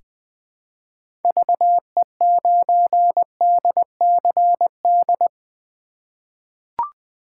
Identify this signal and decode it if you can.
Morse code, heard as a keyed tone, about 21 wpm, 700 Hz. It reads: VE9DCD
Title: K1USN SST QSO Practice - 20wpm